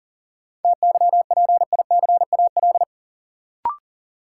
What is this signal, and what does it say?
Morse code, heard as a keyed tone, about 40 wpm, 700 Hz. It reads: TYPICAL
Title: Sentences from Top 1100 Words - New Word Drill Alt - 40wpm